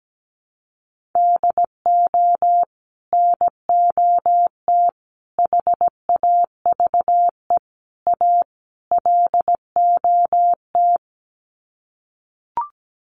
Morse code, heard as a keyed tone, about 17 wpm, 700 Hz. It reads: DO NOT HAVE A LOT
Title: Sets of 5 Words Encore 17wpm